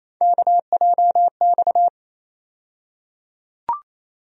Morse code, heard as a keyed tone, about 28 wpm, 700 Hz. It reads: KJX
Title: USA Call Sign Postfixes - 3 Letters 28wpm